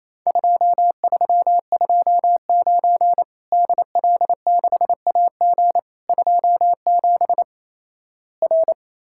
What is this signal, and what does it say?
Morse code, heard as a keyed tone, about 27 wpm, 700 Hz. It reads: 2329 DL6AG 27 R
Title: Worked All Europe DX Contest - QTC Exchange - 28wpm